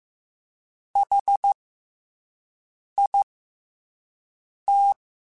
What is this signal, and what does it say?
Morse code, heard as a keyed tone, about 15 wpm, 800 Hz. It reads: H I T
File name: Special INGs-mp3s / Special INGs-mp3s / SlowToFaster ing @10